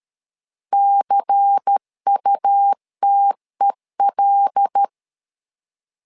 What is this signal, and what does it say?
Morse code, heard as a keyed tone, about 13 wpm, 800 Hz. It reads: CUTEL
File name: LEZIONI / 5 LBVC / 5 LBVC 12wpm 1955